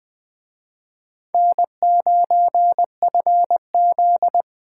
Morse code, heard as a keyed tone, about 24 wpm, 700 Hz. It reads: N9FZ
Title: Speed Racing - USA 1x2 Call Signs 20wpm